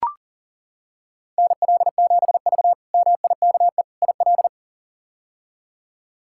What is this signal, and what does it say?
Morse code, heard as a keyed tone, about 37 wpm, 700 Hz. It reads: NL7V MIKE IL
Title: K1USN SST QSO Practice - 40wpm